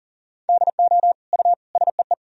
Morse code, heard as a keyed tone, about 40 wpm, 700 Hz. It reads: DO U SEE
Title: Sentences from Top 100 Words and Abbreviations 40wpm